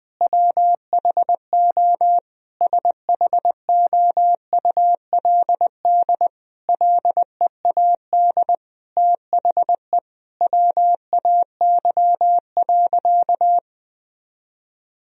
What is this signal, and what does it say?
Morse code, heard as a keyed tone, about 21 wpm, 700 Hz. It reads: WHO SHOULD LEAD THE WAY.
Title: Sentences from Top 1000 Words - Review Part 2 - 20wpm